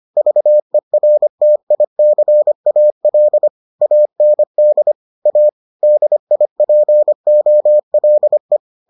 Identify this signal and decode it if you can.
Morse code, heard as a keyed tone, about 25 wpm, 600 Hz. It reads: VERTICAL AND A DIPOLE